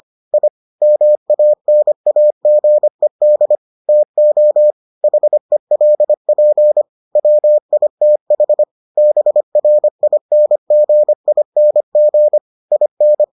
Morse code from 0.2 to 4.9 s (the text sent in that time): I MANAGED TO